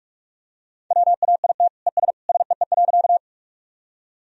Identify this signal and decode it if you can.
Morse code, heard as a keyed tone, about 41 wpm, 700 Hz. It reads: WAIT ES SEE.
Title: Sentences from Top 500 Words and Abbreviations 45wpm